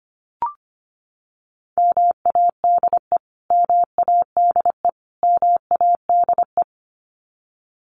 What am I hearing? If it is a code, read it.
Morse code, heard as a keyed tone, about 25 wpm, 700 Hz. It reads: MADE MADE MADE
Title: Top 500 Words Repeated 25wpm